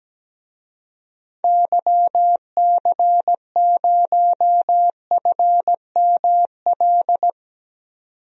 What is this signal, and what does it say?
Morse code, heard as a keyed tone, about 17 wpm, 700 Hz. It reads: YC0FML
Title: DXCC-100 Call Signs - 6 Characters - 17wpm